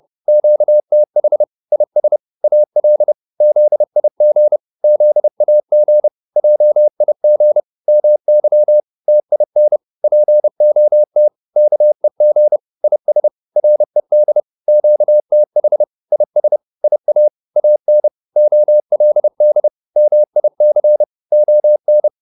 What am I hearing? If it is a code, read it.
Morse code, heard as a keyed tone, about 30 wpm, 600 Hz. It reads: QTH IS AL ZIG ZAG JIG MY TIN POT KEG IS RED QTH IS IA AN OLD MIC ON
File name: PR204 30